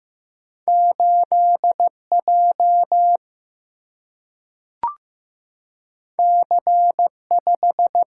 Morse code, heard as a keyed tone, about 15 wpm, 700 Hz. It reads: 8 J C 5
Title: Single-Letter Single-Number 15wpm